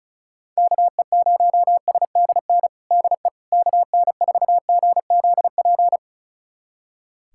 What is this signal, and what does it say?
Morse code, heard as a keyed tone, about 36 wpm, 700 Hz. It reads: KE0SDN DE KN4GZP
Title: USA Callsign DE Callsign - 6 Characters - 35wpm